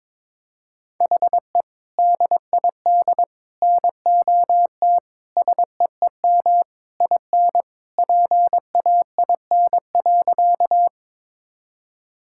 Morse code, heard as a keyed tone, about 21 wpm, 700 Hz. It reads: HE DID NOT SEEM IN PAIN.
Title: Sentences from Top 1600 Words 22wpm